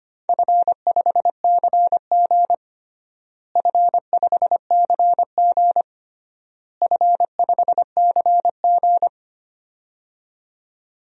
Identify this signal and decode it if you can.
Morse code, heard as a keyed tone, about 25 wpm, 700 Hz. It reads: F5CG F5CG F5CG
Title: DXCC-100 Call Signs - 4 Characters - Repeated 3x - 25wpm